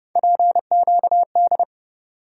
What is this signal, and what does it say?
Morse code, heard as a keyed tone, about 31 wpm, 700 Hz. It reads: PQD
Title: USA Call Sign Postfixes - 3 Letters 30wpm